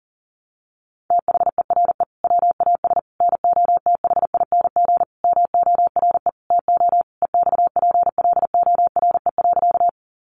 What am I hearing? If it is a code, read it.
Morse code, heard as a keyed tone, about 40 wpm, 700 Hz. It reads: THERE WAS NOTHING MORE TO EXPLORE.
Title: Sentences from Top 1100 Words 40wpm